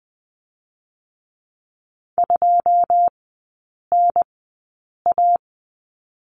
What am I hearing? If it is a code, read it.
Morse code, heard as a keyed tone, about 20 wpm, spaced as for 8 wpm, 690 Hz. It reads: 2NA
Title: 081 Lesson 14e - Number 2 - Three Characters 20wpm